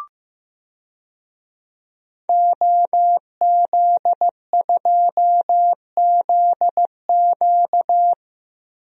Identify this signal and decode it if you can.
Morse code, heard as a keyed tone, about 15 wpm, 700 Hz. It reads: OZ2ZQ
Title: DXCC-100 Call Signs - 5 Characters - 15wpm